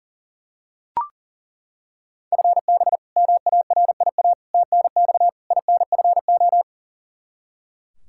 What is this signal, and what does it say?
Morse code, heard as a keyed tone, about 40 wpm, 700 Hz. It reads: FB MARIA TNX INFO
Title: K1USN SST QSO Practice 40wpm